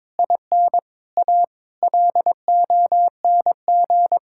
Morse code, heard as a keyed tone, about 22 wpm, 700 Hz. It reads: IN A LONG
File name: practice / 22wpm / Sets of 7 Words - 22wpm